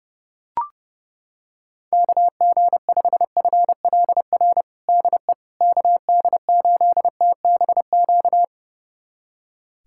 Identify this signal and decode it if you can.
Morse code, heard as a keyed tone, about 30 wpm, 700 Hz. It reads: KG5FLR DE KD8TBQ
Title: USA Callsign DE Callsign 30wpm